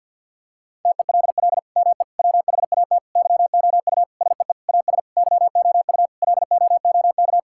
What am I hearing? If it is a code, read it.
Morse code, heard as a keyed tone, about 50 wpm, 700 Hz. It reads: TELL ME WHAT YOU SEE AS YOU LOOK
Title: Sentences from Top 1600 Words 50wpm